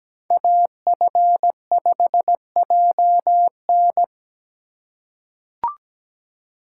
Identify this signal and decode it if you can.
Morse code, heard as a keyed tone, about 17 wpm, 700 Hz. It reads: AF5JN
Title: USA Call Signs - 5 Characters - 4x Character Spacing - 17wpm